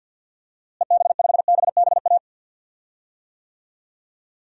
Morse code, heard as a keyed tone, about 51 wpm, 700 Hz. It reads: EB5BBA
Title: DXCC-100 Call Signs - 6 Characters - Repeated 3x - 50wpm